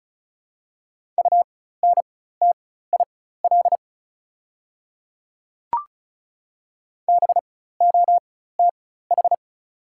Morse code, heard as a keyed tone, about 36 wpm, spaced as for 16 wpm, 690 Hz. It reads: UNTIL BOTH
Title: 4x Character Spacing - Top 300 Words - 35wpm